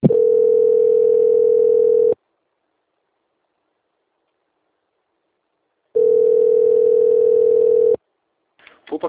Ringback tone (North America), 440 + 480 Hz.